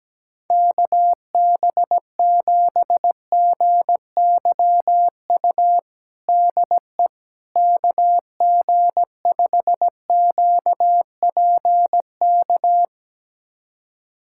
Morse code, heard as a keyed tone, about 17 wpm, 700 Hz. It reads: KB7GYU DE KG5QPK
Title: USA Callsign DE Callsign - 6 Characters - 17wpm